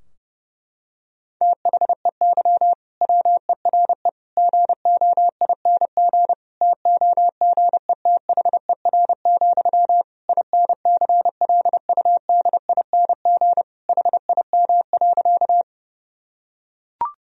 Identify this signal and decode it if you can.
Morse code, heard as a keyed tone, about 32 wpm, 700 Hz. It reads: THEY WERE GOING TOGETHER, INCLUDING HIM.
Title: Sentences from Top 1000 Words - Review Part 4 - 30wpm